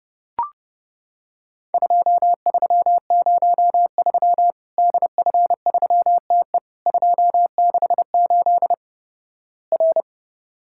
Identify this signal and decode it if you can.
Morse code, heard as a keyed tone, about 30 wpm, 700 Hz. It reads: 2303 DF3TE 268 R
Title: Worked All Europe DX Contest - QTC Exchange - 30wpm